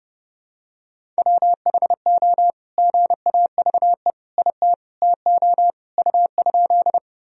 Morse code, heard as a keyed tone, about 30 wpm, 700 Hz. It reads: WHO GAVE IT TO U?
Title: Sentences from Top 500 Words and Abbreviations 30wpm